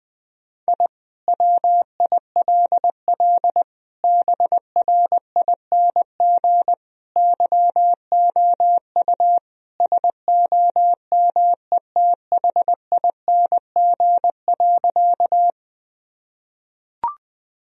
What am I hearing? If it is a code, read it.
Morse code, heard as a keyed tone, about 21 wpm, 700 Hz. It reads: I WILL BRING YOU SOMETHING.
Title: Sentences from Top 1000 Words - Review Part 3 - 20wpm